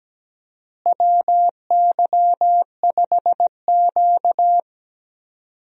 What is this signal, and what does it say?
Morse code, heard as a keyed tone, about 17 wpm, 700 Hz. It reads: WY5Q
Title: USA Call Signs - 4 Characters - Repeated 2x - 17wpm